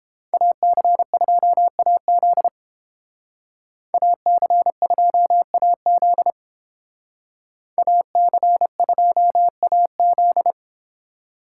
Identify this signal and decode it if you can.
Morse code, heard as a keyed tone, about 30 wpm, 700 Hz. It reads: AC2AZ AC2AZ AC2AZ